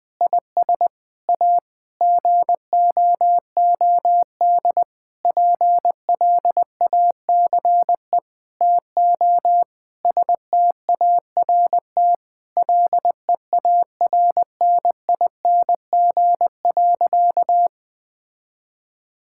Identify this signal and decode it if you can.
Morse code, heard as a keyed tone, about 21 wpm, 700 Hz. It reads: IS A GOOD PLACE TO START LEARNING.
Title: Sentences with 6-Word Phrases - Spoken Beginning - 20wpm